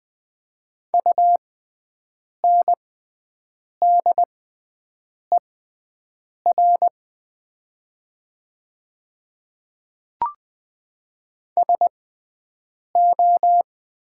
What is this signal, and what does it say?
Morse code, heard as a keyed tone, about 20 wpm, spaced as for 7 wpm, 700 Hz. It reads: UNDER SO
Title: 6x Character Spacing - Top 200 Words - 20wpm